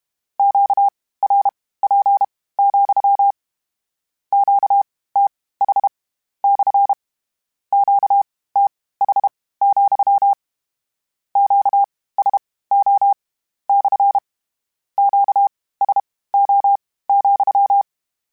Morse code, heard as a keyed tone, about 32 wpm, spaced as for 18 wpm, 800 Hz. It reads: QRP, QTH / QTH, QSO / QSO,
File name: Q Calls / Q Calls-mp3s / Q CALLS SPSL @30